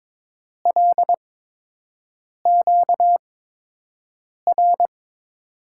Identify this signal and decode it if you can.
Morse code, heard as a keyed tone, about 22 wpm, spaced as for 6 wpm, 700 Hz. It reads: LQR
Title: USA Call Signs - 5 Characters - 8x Character Spacing - 22wpm